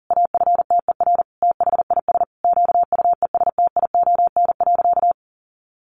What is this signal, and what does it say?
Morse code, heard as a keyed tone, about 40 wpm, 700 Hz. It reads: AFTER THIS QUESTION.
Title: Sentences from Top 1500 Words 40wpm